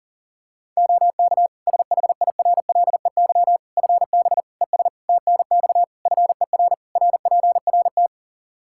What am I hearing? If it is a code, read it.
Morse code, heard as a keyed tone, about 40 wpm, 700 Hz. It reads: OK SHIRLEY FB ES TNX FER RPRT
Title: Practice QSO - Exchange 2 - Info - 40wpm